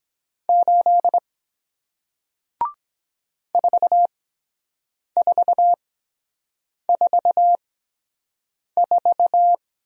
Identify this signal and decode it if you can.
Morse code, heard as a keyed tone, about 23 wpm, spaced as for 7 wpm, 700 Hz. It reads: 8 4444